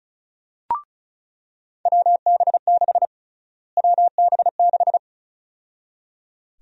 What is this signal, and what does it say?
Morse code, heard as a keyed tone, about 34 wpm, 700 Hz. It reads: WB6 WB6
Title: USA Call Sign Prefixes - Repeated 2x - 35wpm